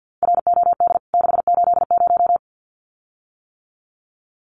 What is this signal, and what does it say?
Morse code, heard as a keyed tone, about 50 wpm, 700 Hz. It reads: ROD 680
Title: CWT Exchange 50wpm